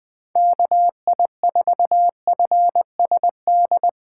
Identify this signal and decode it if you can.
Morse code, heard as a keyed tone, about 21 wpm, 700 Hz. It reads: KI4FSD
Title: USA Call Signs - 6 Characters - 2x Character Spacing - 20wpm